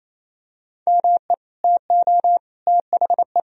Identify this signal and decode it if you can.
Morse code, heard as a keyed tone, about 28 wpm, 700 Hz. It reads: ME TO THE
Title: Sentences from Top 400 Words and Abbreviations 28wpm